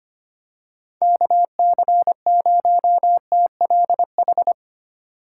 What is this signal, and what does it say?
Morse code, heard as a keyed tone, about 25 wpm, 700 Hz. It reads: KC0TLH
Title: USA Call Signs - 6 Characters - 4x Character Spacing - 25wpm